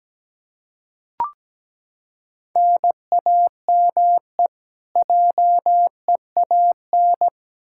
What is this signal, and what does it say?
Morse code, heard as a keyed tone, about 17 wpm, 700 Hz. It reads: NAME JEAN
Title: Practice QSO - Exchange 1 - Beginning - 17wpm